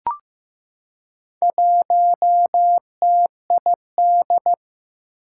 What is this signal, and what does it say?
Morse code, heard as a keyed tone, about 15 wpm, 700 Hz. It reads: 1TID
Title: Four Challenging Letter-Number Combinations 15wpm